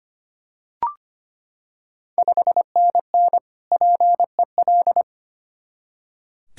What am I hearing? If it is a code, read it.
Morse code, heard as a keyed tone, about 25 wpm, 700 Hz. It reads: 5NN PEL
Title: Ontario QSO Party Exchange 25wpm